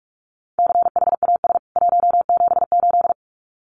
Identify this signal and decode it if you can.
Morse code, heard as a keyed tone, about 44 wpm, 700 Hz. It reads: CHAS 178
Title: CWT Exchange 45wpm